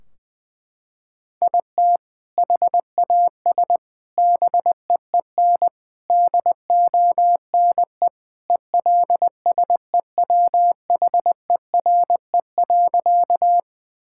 Morse code, heard as a keyed tone, about 21 wpm, 700 Hz. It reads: IT HAS BEEN DONE ELSEWHERE.
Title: Sentences from Top 1600 Words 20wpm